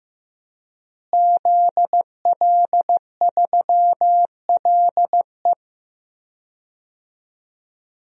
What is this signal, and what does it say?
Morse code, heard as a keyed tone, about 15 wpm, 700 Hz. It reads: ZL3LE
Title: DXCC-100 Call Signs - 5 Characters - Repeated 2x - 15wpm